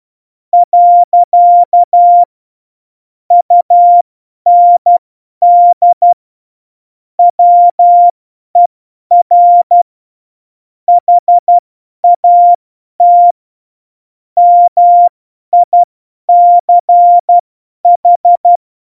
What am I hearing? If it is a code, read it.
Morse code, heard as a keyed tone, about 12 wpm, spaced as for 10 wpm, 700 Hz. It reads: . UND WER HAT MICH
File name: Morsekurs / Witze / Witz 03 April 2018, 50 F70